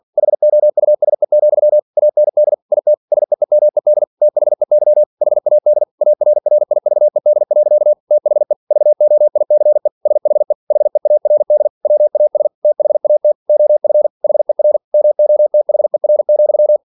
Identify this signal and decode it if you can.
Morse code, heard as a keyed tone, about 46 wpm, 600 Hz. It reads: HOUSE, AND IT SEEMED THEY HAD ARRIVED. THE VOICE SHE HEARD WAS THAT OF HER MOTHER,
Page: Never 47wpm